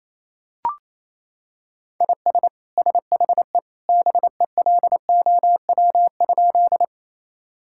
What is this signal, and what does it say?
Morse code, heard as a keyed tone, about 28 wpm, 700 Hz. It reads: IS SHE BELOW?
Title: Sentences from Top 1000 Words - Review Part 5 - 28wpm